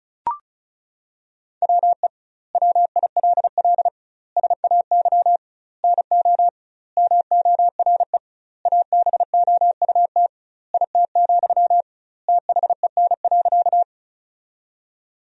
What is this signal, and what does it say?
Morse code, heard as a keyed tone, about 36 wpm, 700 Hz. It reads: WE WILL SAY NO MORE ABOUT IT, THEN.
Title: 2x Word-Spacing - Sentences from Top 200 Words - 35wpm